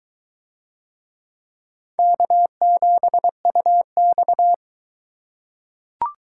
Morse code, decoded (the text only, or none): K7UX